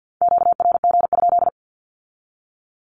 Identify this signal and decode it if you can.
Morse code, heard as a keyed tone, about 50 wpm, 700 Hz. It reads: QRG?
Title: Q-Codes with Spoken Meaning - 50wpm